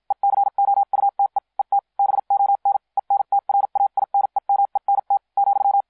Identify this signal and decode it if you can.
Morse code, heard as a keyed tone, about 56 wpm, spaced as for 45 wpm, 800 Hz. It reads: ECOUTE ET BON ENTRAINEMENT ,